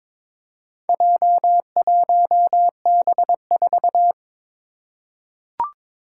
Morse code, heard as a keyed tone, about 22 wpm, 700 Hz. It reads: J1B4
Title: Four Challenging Letter-Number Combinations 22wpm